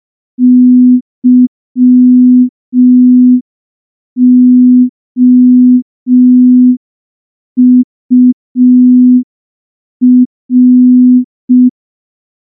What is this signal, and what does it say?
Morse code, heard as a keyed tone, about 5 wpm, 250 Hz. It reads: YOUR
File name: morse / morse-training-05WPM